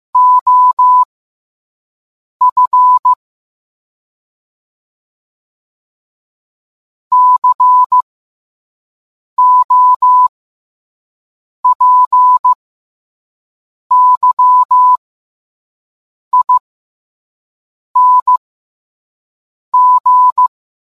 Morse code, heard as a keyed tone, about 15 wpm, spaced as for 5 wpm, 1000 Hz. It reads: OF COPYING